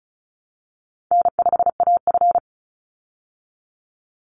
Morse code, heard as a keyed tone, about 34 wpm, 700 Hz. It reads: N5AF
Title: Easy Call Signs Repeated 3x - 35wpm